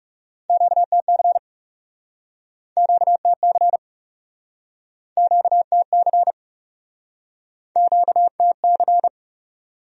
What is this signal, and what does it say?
Morse code, heard as a keyed tone, about 42 wpm, 700 Hz. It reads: QTC QTC QTC QTC